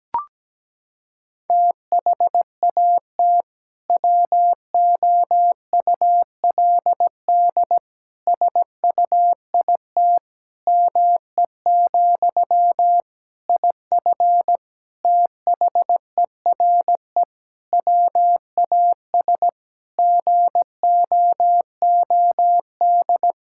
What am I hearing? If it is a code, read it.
Morse code, heard as a keyed tone, about 17 wpm, 700 Hz. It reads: THAT WOULD SUIT ME, IF THERE WAS GOOD
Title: Sentences from Top 1600 Words 17wpm